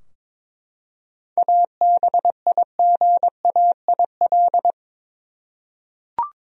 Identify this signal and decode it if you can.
Morse code, heard as a keyed tone, about 23 wpm, 700 Hz. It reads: ABIGAIL
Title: USA Common Names - 22wpm